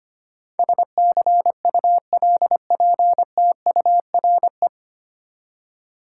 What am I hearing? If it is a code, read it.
Morse code, heard as a keyed tone, about 25 wpm, 700 Hz. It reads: SCULPTURE
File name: Nine Letter Words - 25wpm